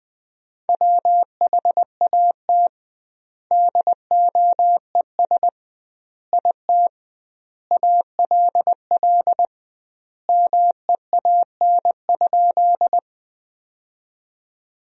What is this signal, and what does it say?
Morse code, heard as a keyed tone, about 20 wpm, 700 Hz. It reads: WHAT DOES IT ALL MEAN?
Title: 2x Word-Spacing - Sentences from Top 200 Words - 20wpm